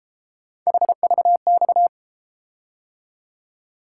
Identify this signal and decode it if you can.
Morse code, heard as a keyed tone, about 32 wpm, 700 Hz. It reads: HVX